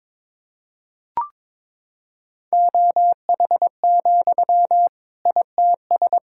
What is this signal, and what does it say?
Morse code, heard as a keyed tone, about 22 wpm, 700 Hz. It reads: OH, ITS
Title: Sentences from Top 1000 Words - Review Part 6 - 22wpm